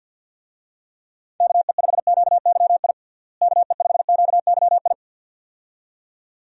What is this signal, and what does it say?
Morse code, heard as a keyed tone, about 52 wpm, 700 Hz. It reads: KE5QYI KE5QYI
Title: USA Call Signs - 6 Characters - Repeated 2x - 50wpm